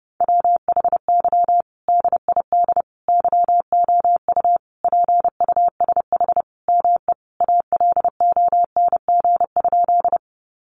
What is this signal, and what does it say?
Morse code, heard as a keyed tone, about 30 wpm, 700 Hz. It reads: WHY DID YOU PUSH ME ALONG?
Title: Sentences from Top 1300 Words 30wpm